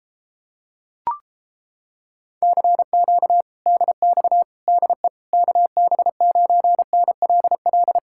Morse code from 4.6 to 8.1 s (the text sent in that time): DE KB9NLL